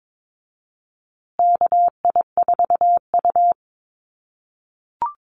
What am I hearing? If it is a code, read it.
Morse code, heard as a keyed tone, about 22 wpm, 700 Hz. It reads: KI4U
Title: USA Call Signs Repeated 3x 22wpm